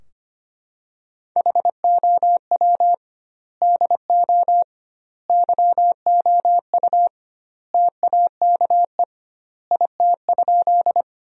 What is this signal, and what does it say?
Morse code, heard as a keyed tone, about 25 wpm, 700 Hz. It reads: HOW DO YOU TAKE IT?
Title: 2x Word-Spacing - Sentences from Top 200 Words - 25wpm